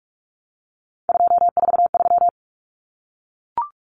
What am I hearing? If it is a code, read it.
Morse code, heard as a keyed tone, about 44 wpm, 700 Hz. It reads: 243
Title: Numbers - 3-Digit 45wpm